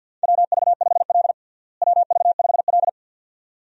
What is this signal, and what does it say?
Morse code, heard as a keyed tone, about 52 wpm, 700 Hz. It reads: WV5L WV5L
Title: USA Call Signs - 4 Characters - Repeated 2x - 50wpm